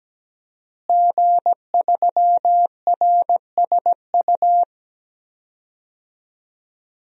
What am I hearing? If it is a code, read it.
Morse code, heard as a keyed tone, about 17 wpm, 700 Hz. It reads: G3RSU
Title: DXCC-100 Call Signs - 5 Characters - Repeated 2x - 17wpm